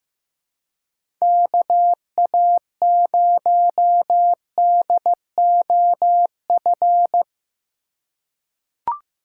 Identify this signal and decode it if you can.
Morse code, heard as a keyed tone, about 15 wpm, 700 Hz. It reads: KA0DOF
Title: USA Call Signs - 6 Characters - 3x Character Spacing - 15wpm